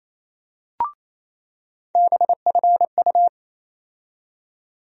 Morse code, heard as a keyed tone, about 28 wpm, 700 Hz. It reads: BFU
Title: USA Call Sign Postfixes - 3 Letters 28wpm